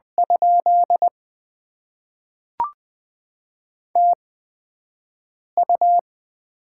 Morse code, heard as a keyed tone, about 20 wpm, spaced as for 5 wpm, 700 Hz. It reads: ? TU
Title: Top 20 QSO Elements - 8x Character Spacing - 20wpm